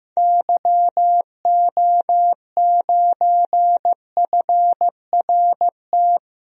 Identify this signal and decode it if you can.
Morse code, heard as a keyed tone, about 15 wpm, 700 Hz. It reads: YO9FRT
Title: DXCC-100 Call Signs - 6 Characters - Repeated 3x - 15wpm